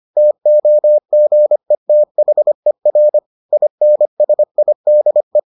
Morse code, heard as a keyed tone, about 25 wpm, 600 Hz. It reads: TOGETHER INSIDE